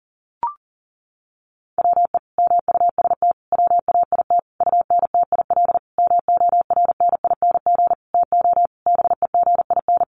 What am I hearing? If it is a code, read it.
Morse code, heard as a keyed tone, about 40 wpm, 700 Hz. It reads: WE MUST WAIT UNTIL MORNING TO BEGIN
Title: Sentences from Top 1300 Words 40wpm